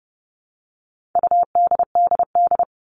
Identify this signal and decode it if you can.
Morse code, heard as a keyed tone, about 30 wpm, 700 Hz. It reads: UDDD
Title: All Cut Numbers - 4-Digit 30wpm